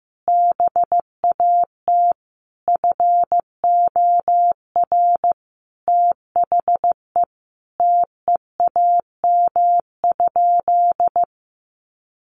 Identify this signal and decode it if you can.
Morse code, heard as a keyed tone, about 15 wpm, 700 Hz. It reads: BAT FOR THE TEAM?
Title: Sentences from Top 1500 Words 15wpm